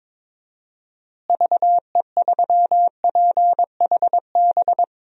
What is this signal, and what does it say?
Morse code, heard as a keyed tone, about 21 wpm, 700 Hz. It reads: VE3PHB
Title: DXCC-100 Call Signs - 6 Characters - Repeated 3x - 22wpm